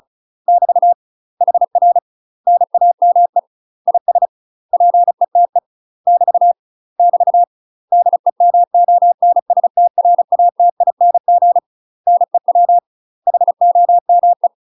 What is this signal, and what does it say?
Morse code, heard as a keyed tone, about 35 wpm, 700 Hz. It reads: <BT> HR NAME IS PETE <BT> <BT> DEMONSTRATING NEW HOME
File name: UK MASTER QSOs TRAINING / 35WPM / UK MASTER QSOs 0015